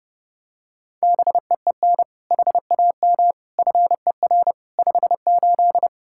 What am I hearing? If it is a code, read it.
Morse code, heard as a keyed tone, about 30 wpm, 700 Hz. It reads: BEEN HAM FER 58
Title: Practice QSO - Exchange 3 - More Info - 30wpm